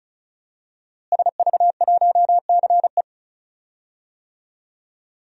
Morse code, heard as a keyed tone, about 36 wpm, 700 Hz. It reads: SV1CE
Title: DXCC-100 Call Signs - 5 Characters - Repeated 2x - 35wpm